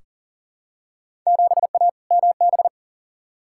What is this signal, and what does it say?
Morse code, heard as a keyed tone, about 37 wpm, 700 Hz. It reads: 7A MB
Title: Field Day Exchanges Encore - 40wpm